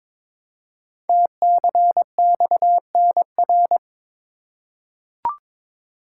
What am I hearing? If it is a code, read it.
Morse code, heard as a keyed tone, about 22 wpm, 700 Hz. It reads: TCXNR
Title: Code Groups - 5 Letters - 22wpm